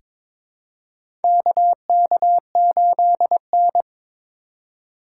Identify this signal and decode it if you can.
Morse code, heard as a keyed tone, about 21 wpm, 700 Hz. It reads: KK8N
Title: USA Call Signs - 4 Characters - 4x Character Spacing - 22wpm